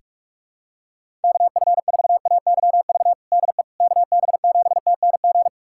Morse code, heard as a keyed tone, about 46 wpm, 700 Hz. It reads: KF4AYV DE KB7TNG
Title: USA Callsign DE Callsign - 6 Characters - 45wpm